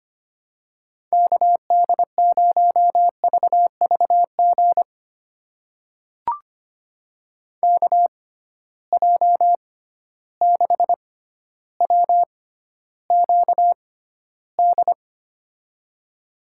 Morse code, heard as a keyed tone, about 25 wpm, spaced as for 13 wpm, 700 Hz. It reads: KD0VVG KJ6WQD
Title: USA Call Signs - 6 Characters - 6x Character Spacing - 25wpm